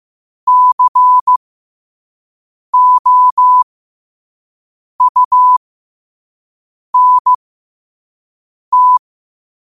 Morse code, heard as a keyed tone, about 15 wpm, spaced as for 5 wpm, 1000 Hz. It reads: COUNT
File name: qotdfiles / 5wpm